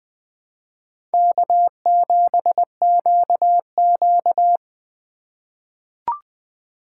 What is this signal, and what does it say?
Morse code, heard as a keyed tone, about 20 wpm, 700 Hz. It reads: K7QQ
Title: USA Call Signs - 4 Characters - 6x Character Spacing - 20wpm